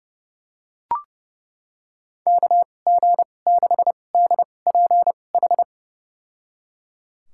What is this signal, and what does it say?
Morse code, heard as a keyed tone, about 31 wpm, spaced as for 22 wpm, 700 Hz. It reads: KG6DPH
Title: USA Call Signs - 6 Characters - 2x Character Spacing - 30wpm